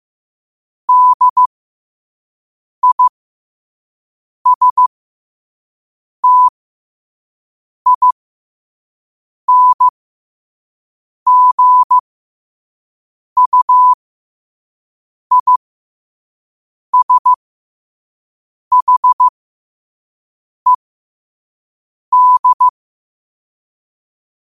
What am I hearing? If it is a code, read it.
Morse code, heard as a keyed tone, about 15 wpm, spaced as for 5 wpm, 1000 Hz. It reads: DISTINGUISHED
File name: qotdfiles / 5wpm